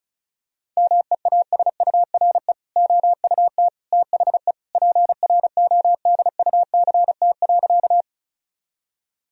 Morse code, heard as a keyed tone, about 36 wpm, 700 Hz. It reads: MEASURE OUT THE PRODUCT.
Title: Sentences from Top 1000 Words - Review Part 2 - 35wpm